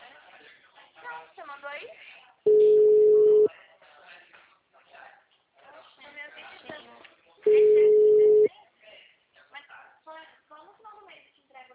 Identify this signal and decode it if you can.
Ringback tone (Europe, 425 Hz), with speech.